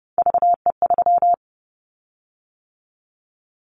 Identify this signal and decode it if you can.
Morse code, heard as a keyed tone, about 30 wpm, 700 Hz. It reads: VE3
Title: Callsign Prefixes for Top 25 DXCC - 30wpm